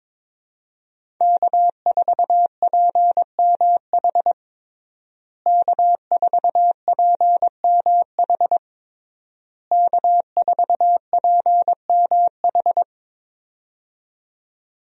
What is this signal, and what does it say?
Morse code, heard as a keyed tone, about 22 wpm, 700 Hz. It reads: K4PMH K4PMH K4PMH
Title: USA Call Signs - 5 Characters - Repeated 3x - 22wpm